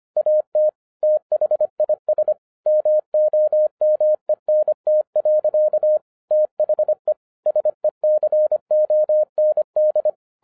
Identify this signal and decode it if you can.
Morse code, heard as a keyed tone, about 25 wpm, 610 Hz. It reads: AT THIS MOMENT. THE SECOND